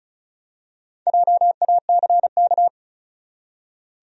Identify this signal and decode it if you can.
Morse code, heard as a keyed tone, about 36 wpm, 700 Hz. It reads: JACK
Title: USA Common Names - Max 4 Letters - 35wpm